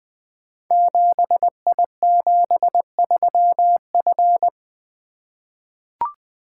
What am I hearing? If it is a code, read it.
Morse code, heard as a keyed tone, about 20 wpm, 700 Hz. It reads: 7I73F
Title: Code Groups - 5 Letter:Numbers - 20wpm